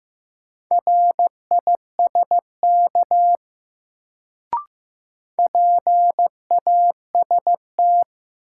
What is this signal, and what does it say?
Morse code, heard as a keyed tone, about 15 wpm, 700 Hz. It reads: RISK PAST
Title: Sentences from Top 600 Words - New Word Drill - 15wpm